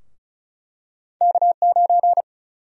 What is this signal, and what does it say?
Morse code, heard as a keyed tone, about 36 wpm, 700 Hz. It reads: K9
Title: USA Call Sign Prefixes - Repeated 2x - 35wpm